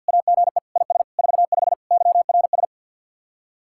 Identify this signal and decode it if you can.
Morse code, heard as a keyed tone, about 61 wpm, 700 Hz. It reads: AGE IS 45 YRS
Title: Speed-Racing Common QSO Phrases Encore - 50wpm